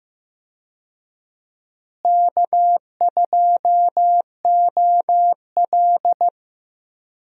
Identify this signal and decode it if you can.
Morse code, heard as a keyed tone, about 15 wpm, 700 Hz. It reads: K2OL
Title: USA Call Signs - 4 Characters - Repeated 3x - 15wpm